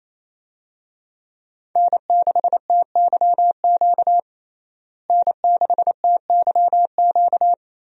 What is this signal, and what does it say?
Morse code, heard as a keyed tone, about 28 wpm, 700 Hz. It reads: N6TYQ N6TYQ
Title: USA Call Signs - 5 Characters - Repeated 2x - 28wpm